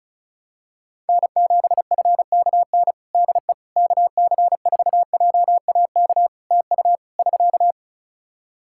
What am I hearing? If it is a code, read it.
Morse code, heard as a keyed tone, about 36 wpm, 700 Hz. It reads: N7FKN DE KC4JAK TU <SK>
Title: Practice QSO - Exchange 4 - Ending - 35wpm